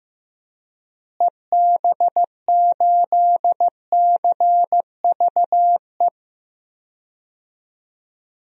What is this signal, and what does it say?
Morse code, heard as a keyed tone, about 16 wpm, 700 Hz. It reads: EB8CVE
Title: DXCC-100 Call Signs - 6 Characters - Repeated 3x - 15wpm